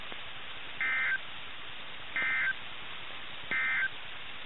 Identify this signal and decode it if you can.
Emergency Alert System (SAME) tones, heard as harsh buzzing bursts. It is heard from 0.7 to 4.0 s.
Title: NOAA Weather Radio - KIH61